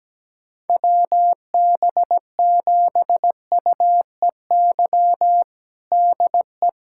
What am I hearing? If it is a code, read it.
Morse code, heard as a keyed tone, about 17 wpm, 700 Hz. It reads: WB7UEY DE
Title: USA Callsign DE Callsign - 6 Characters - 17wpm